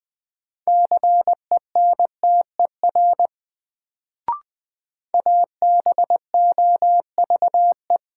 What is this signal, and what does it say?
Morse code, heard as a keyed tone, about 20 wpm, 700 Hz. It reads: CENTER ABOVE
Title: Sentences from Top 1000 Words - Review Part 2 - New Word Drill - 20wpm